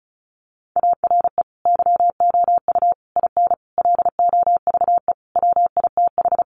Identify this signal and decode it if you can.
Morse code, heard as a keyed tone, about 34 wpm, 700 Hz. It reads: ARE YOU IN LOVE WITH
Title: Sentences from Top 400 Words 35wpm